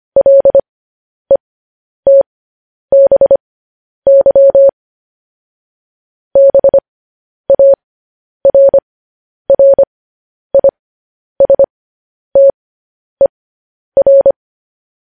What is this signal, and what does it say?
Morse code, heard as a keyed tone, about 25 wpm, spaced as for 10 wpm, 550 Hz. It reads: LETBY BARRISTER
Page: news headlines in morse code at 10 wpm